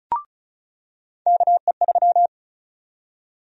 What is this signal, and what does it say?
Morse code, heard as a keyed tone, about 34 wpm, 700 Hz. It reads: KE3
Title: USA Call Sign Prefixes - 35wpm